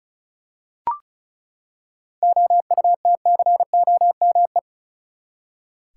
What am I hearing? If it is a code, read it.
Morse code, heard as a keyed tone, about 34 wpm, 700 Hz. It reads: OUTCOME
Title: Seven Letter Words - 35wpm